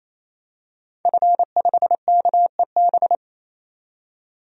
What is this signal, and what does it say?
Morse code, heard as a keyed tone, about 28 wpm, 700 Hz. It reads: F5KEB
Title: DXCC-100 Call Signs - 5 Characters - 28wpm